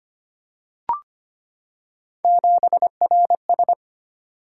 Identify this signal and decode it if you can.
Morse code, heard as a keyed tone, about 25 wpm, 700 Hz. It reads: 7RS
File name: Three Letter-Number Combinations - 25wpm